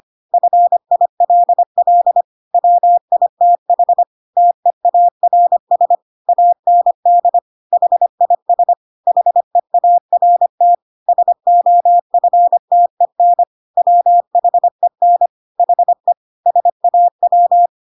Morse code, heard as a keyed tone, about 25 wpm, 700 Hz. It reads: FILL WITH TEARS AND HIS HEART SOFTEN WHEN HE SAW
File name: Pinocchio / Pinocchio ch8 25 25 ik1hge